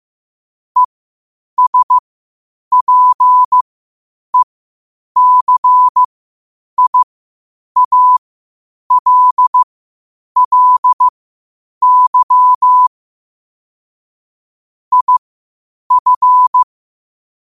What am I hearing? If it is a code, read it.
Morse code, heard as a keyed tone, about 15 wpm, spaced as for 8 wpm, 1000 Hz. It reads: ESPECIALLY IF
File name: qotdfiles / 8wpm